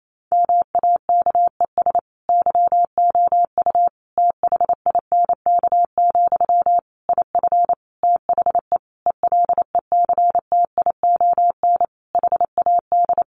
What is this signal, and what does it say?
Morse code, heard as a keyed tone, about 28 wpm, 700 Hz. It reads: MAKES YOU THINK, IF THE ELECTION HAD
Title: A Sound Of Thunder 28wpm